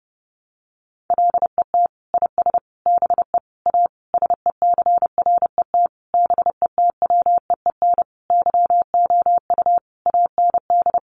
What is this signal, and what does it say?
Morse code, heard as a keyed tone, about 30 wpm, 700 Hz. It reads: LET IS BE A SECRET BETWEEN YOU AND
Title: Sentences from Top 1200 Words 30wpm